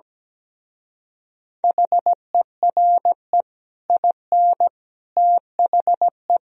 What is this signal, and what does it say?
Morse code, heard as a keyed tone, about 17 wpm, 700 Hz. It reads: HERE IN THE
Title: Speed-Racing - Sets of 5 Words 17wpm